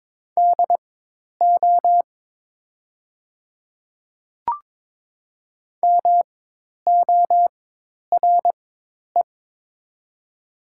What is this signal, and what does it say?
Morse code, heard as a keyed tone, about 22 wpm, spaced as for 10 wpm, 700 Hz. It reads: DO MORE
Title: 4x Character Spacing - Top 100 Words - 22wpm